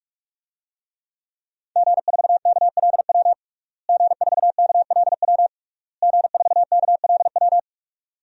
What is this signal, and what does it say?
Morse code, heard as a keyed tone, about 46 wpm, 700 Hz. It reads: G4KLW G4KLW G4KLW
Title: DXCC-100 Call Signs - 5 Characters - Repeated 3x - 45wpm